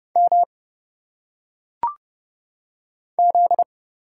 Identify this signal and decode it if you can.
Morse code, heard as a keyed tone, about 30 wpm, 700 Hz. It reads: M Z
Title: Single Letters - Phonetic Pronunciation 30wpm